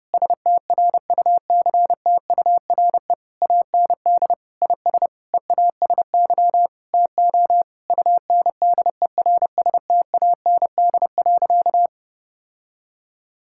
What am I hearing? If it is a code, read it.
Morse code, heard as a keyed tone, about 30 wpm, 700 Hz. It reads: STRUCTURE AND IS EASY TO UNDERSTAND.
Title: Sentences from Top 1000 Words - Review Part 6 - 30wpm